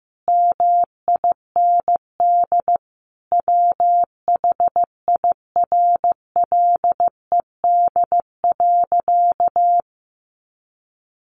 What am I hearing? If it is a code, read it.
Morse code, heard as a keyed tone, about 15 wpm, 700 Hz. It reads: MIND WHIRLED.
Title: A Sound Of Thunder 15wpm